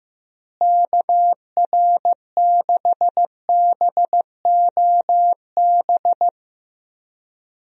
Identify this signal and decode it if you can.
Morse code, heard as a keyed tone, about 14 wpm, 700 Hz. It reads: KR6BOB
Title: USA Call Signs - 6 Characters - Repeated 3x - 15wpm